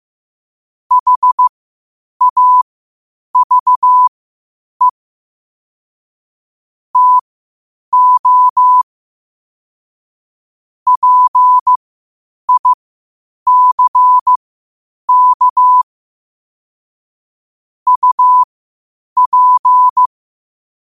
Morse code, heard as a keyed tone, about 15 wpm, spaced as for 8 wpm, 1000 Hz. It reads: HAVE TO PICK UP
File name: qotdfiles / 8wpm